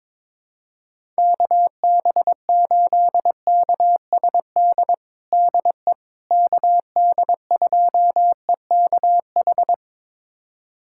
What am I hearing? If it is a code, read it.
Morse code, heard as a keyed tone, about 22 wpm, 700 Hz. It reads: KB8KSD DE KD2EKH
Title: USA Callsign DE Callsign 22wpm